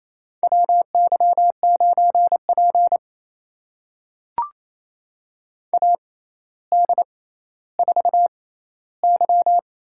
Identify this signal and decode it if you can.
Morse code, heard as a keyed tone, about 28 wpm, spaced as for 14 wpm, 700 Hz. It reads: WY9P AD4Y
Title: USA Call Signs - 6x Character Spacing - 28wpm